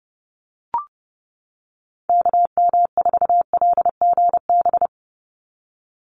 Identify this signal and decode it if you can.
Morse code, heard as a keyed tone, about 30 wpm, 700 Hz. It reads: KM4LGB
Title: USA Call Signs 30wpm